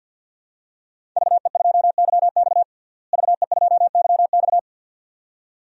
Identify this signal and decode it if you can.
Morse code, heard as a keyed tone, about 51 wpm, 700 Hz. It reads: VE2YX VE2YX
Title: DXCC-100 Call Signs - 5 Characters - Repeated 2x - 50wpm